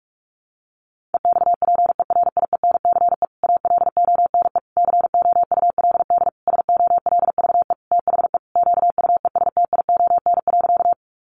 Morse code, heard as a keyed tone, about 44 wpm, 700 Hz. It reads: EXPERIENCE ALONE COULD SOLVE THE QUESTION.
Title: Sentences from Top 1000 Words 45wpm